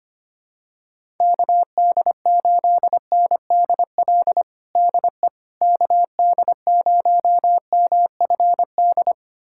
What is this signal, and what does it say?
Morse code, heard as a keyed tone, about 25 wpm, 700 Hz. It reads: KD8NDL DE KD0MFD
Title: USA Callsign DE Callsign 25wpm